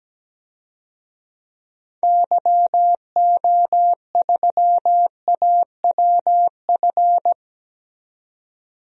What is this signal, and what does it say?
Morse code, heard as a keyed tone, about 17 wpm, 700 Hz. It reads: YO3AWF
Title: DXCC-100 Call Signs - 6 Characters - 17wpm